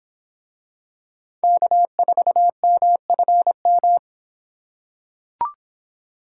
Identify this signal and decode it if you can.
Morse code, heard as a keyed tone, about 25 wpm, 700 Hz. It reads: K4MFM